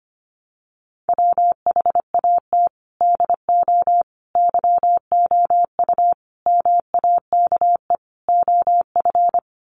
Morse code, heard as a keyed tone, about 25 wpm, 700 Hz. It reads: WHAT DO YOU MAKE OF
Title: Sets of 5 Words 25wpm